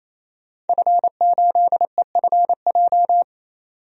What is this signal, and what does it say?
Morse code, heard as a keyed tone, about 28 wpm, 700 Hz. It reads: F8EFJ
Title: DXCC-100 Call Signs - 5 Characters - 28wpm